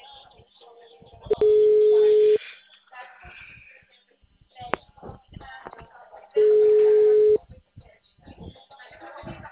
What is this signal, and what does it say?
Ringback tone (Europe, 425 Hz), with speech.